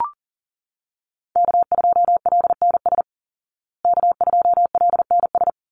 Morse code, heard as a keyed tone, about 37 wpm, 700 Hz. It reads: K2LNS K2LNS
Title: Call Signs Repeated 3x - 40wpm